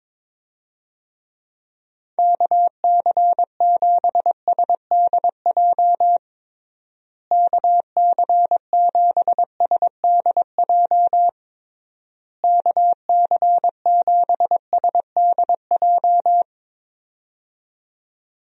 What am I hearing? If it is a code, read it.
Morse code, heard as a keyed tone, about 22 wpm, 700 Hz. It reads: KC7SDJ KC7SDJ KC7SDJ
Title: USA Call Signs - 6 Characters - Repeated 3x - 22wpm